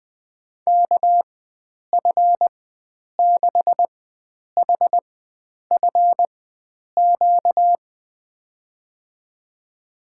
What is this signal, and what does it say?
Morse code, heard as a keyed tone, about 20 wpm, spaced as for 9 wpm, 700 Hz. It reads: KF6HFQ
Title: USA Call Signs - 6 Characters - 4x Character Spacing - 20wpm